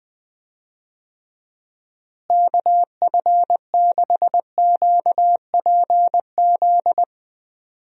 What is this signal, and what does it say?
Morse code, heard as a keyed tone, about 20 wpm, 700 Hz. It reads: KF6QPZ
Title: USA Call Signs - 6 Characters - Repeated 3x - 20wpm